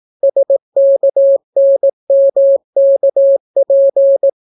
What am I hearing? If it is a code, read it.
Morse code, heard as a keyed tone, about 18 wpm, 550 Hz. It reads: SKNMKP